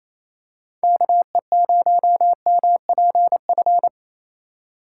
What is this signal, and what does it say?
Morse code, heard as a keyed tone, about 28 wpm, 700 Hz. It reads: KE0MPF
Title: USA Call Signs - 4x Character Spacing - 28wpm